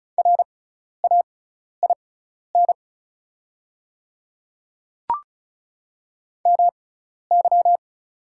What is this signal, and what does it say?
Morse code, heard as a keyed tone, about 36 wpm, spaced as for 12 wpm, 700 Hz. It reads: RAIN MY
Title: 6x Character Spacing - Top 300 Words - 35wpm